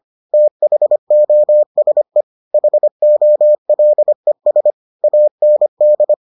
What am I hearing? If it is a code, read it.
Morse code, heard as a keyed tone, about 25 wpm, 600 Hz. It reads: THOSE HOLES AND